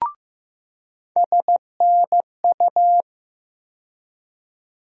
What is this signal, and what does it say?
Morse code, heard as a keyed tone, about 16 wpm, 700 Hz. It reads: SNU
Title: USA Call Sign Postfixes - 3 Letters 15wpm